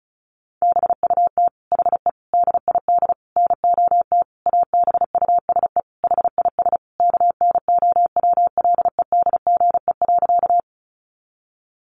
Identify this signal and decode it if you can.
Morse code, heard as a keyed tone, about 34 wpm, 700 Hz. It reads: BUT HE DID NOT ABUSE HIS KNOWLEDGE.
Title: Sentences from Top 1300 Words 35wpm